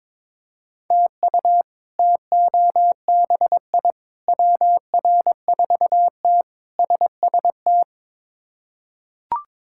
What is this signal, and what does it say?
Morse code, heard as a keyed tone, about 22 wpm, 700 Hz. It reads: TU TOBI WR4T SST
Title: K1USN SST QSO Practice - 22wpm